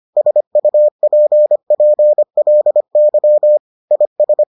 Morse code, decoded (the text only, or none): SUPPLY IS